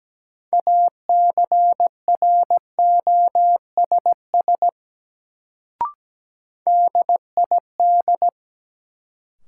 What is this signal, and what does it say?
Morse code, heard as a keyed tone, about 17 wpm, 700 Hz. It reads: ACROSS DID
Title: Sets of 2 Words - New Word Drill - 17wpm